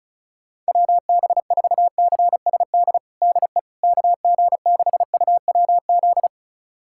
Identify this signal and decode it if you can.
Morse code, heard as a keyed tone, about 36 wpm, 700 Hz. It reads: WB4CSD DE KG6UWZ
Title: USA Callsign DE Callsign 35wpm